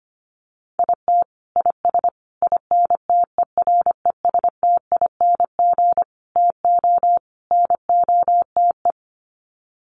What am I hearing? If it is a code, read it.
Morse code, heard as a keyed tone, about 25 wpm, 700 Hz. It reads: IT IS INTERESTING TO NOTE
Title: Sets of 5 Words Encore2 25wpm